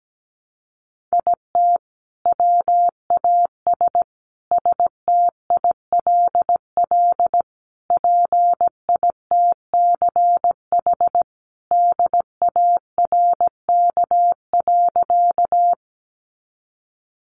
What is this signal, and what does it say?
Morse code, heard as a keyed tone, about 17 wpm, 700 Hz. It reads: IT WAS STILL PITCH DARK.
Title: Sentences from Top 1300 Words 17wpm